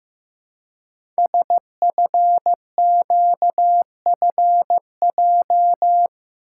Morse code, heard as a keyed tone, about 15 wpm, 700 Hz. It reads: SFQFJ
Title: Code Groups - 5 Letters - 15wpm